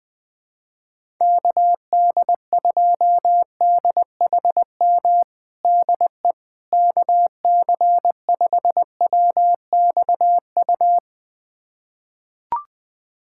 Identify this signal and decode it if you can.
Morse code, heard as a keyed tone, about 20 wpm, 700 Hz. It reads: KD2DHM DE KC5WXU
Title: USA Callsign DE Callsign - 6 Characters - 20wpm